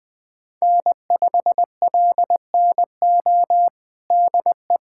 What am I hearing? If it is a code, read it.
Morse code, heard as a keyed tone, about 20 wpm, 700 Hz. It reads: N5LNO DE
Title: USA Callsign DE Callsign - 5 Characters - 20wpm